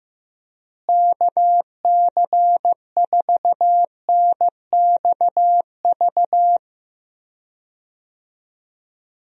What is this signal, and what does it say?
Morse code, heard as a keyed tone, about 16 wpm, 700 Hz. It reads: KC4NXV
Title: USA Call Signs - 6 Characters - Repeated 3x - 15wpm